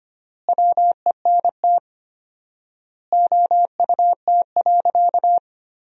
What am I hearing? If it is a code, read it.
Morse code, heard as a keyed tone, about 25 wpm, 700 Hz. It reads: WENT OUT.
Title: 4x Word-Spacing - Sentences from Top 300 Words - 25wpm